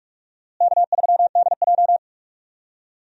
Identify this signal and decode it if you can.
Morse code, heard as a keyed tone, about 46 wpm, 700 Hz. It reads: K3DJ
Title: USA Call Signs - 4 Characters - Repeated 2x - 45wpm